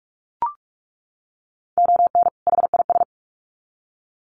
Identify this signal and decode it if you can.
Morse code, heard as a keyed tone, about 44 wpm, 700 Hz. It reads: ON HIS
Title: Sets of 2 Words 45wpm